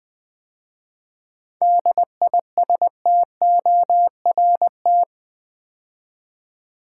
Morse code, heard as a keyed tone, about 20 wpm, 700 Hz. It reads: DISTORT
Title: 7 Letter Words - First 3 Letters Given - 20wpm